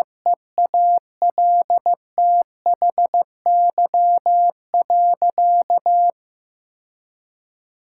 Morse code, heard as a keyed tone, about 15 wpm, 700 Hz. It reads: EALTHY.
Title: Sentences from Top 1000 Words - Review Part 5 - 15wpm